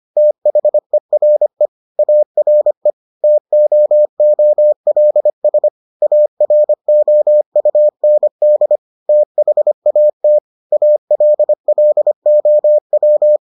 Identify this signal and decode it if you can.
Morse code, heard as a keyed tone, about 25 wpm, 600 Hz. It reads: THERE ARE TOOLS AROUND THAT ALLOW